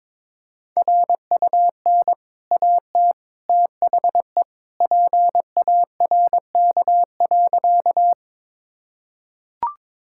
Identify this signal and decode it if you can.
Morse code, heard as a keyed tone, about 22 wpm, 690 Hz. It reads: RUN AT THE PARK.
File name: practice / 22wpm / Easy Sentences with Hints - 22wpm